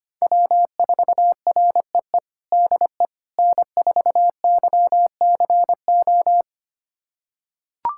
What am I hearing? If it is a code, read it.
Morse code, heard as a keyed tone, about 25 wpm, 700 Hz. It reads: W4REE DE N4YCO
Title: USA Callsign DE Callsign - 5 Characters - 25wpm